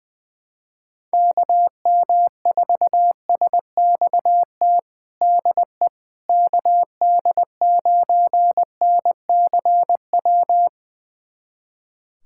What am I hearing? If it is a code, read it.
Morse code, heard as a keyed tone, about 20 wpm, 700 Hz. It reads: KM4SXT DE KD9NCW
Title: USA Callsign DE Callsign - 6 Characters - 20wpm